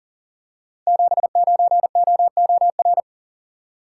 Morse code, heard as a keyed tone, about 40 wpm, 700 Hz. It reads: 79OOR
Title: Code Groups - 5 Letter:Numbers - 40wpm